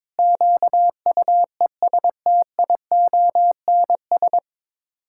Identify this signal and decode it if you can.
Morse code, heard as a keyed tone, about 21 wpm, 700 Hz. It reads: QUESTIONS
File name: practice / 22wpm / Sets of 8 Words - 22wpm